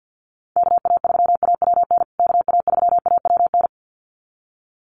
Morse code, heard as a keyed tone, about 48 wpm, 700 Hz. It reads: KA3AWN KA3AWN
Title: USA Call Signs Repeated 2x 50wpm